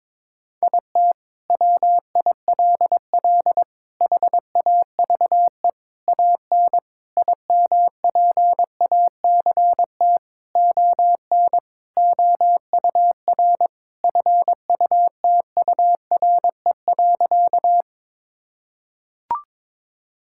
Morse code, heard as a keyed tone, about 21 wpm, 700 Hz. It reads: IT WILL HAVE AN IMPACT ON OUR FUTURE.
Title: Sentences from Top 1000 Words - Review Part 6 - 22wpm